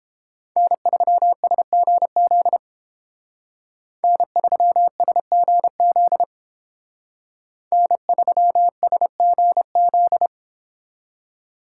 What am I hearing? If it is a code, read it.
Morse code, heard as a keyed tone, about 31 wpm, 700 Hz. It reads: N3SGZ N3SGZ N3SGZ